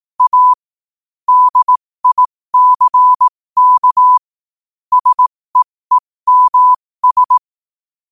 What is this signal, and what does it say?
Morse code, heard as a keyed tone, about 18 wpm, spaced as for 15 wpm, 1000 Hz. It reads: A DICK SEEMS
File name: qotdfiles / 15wpm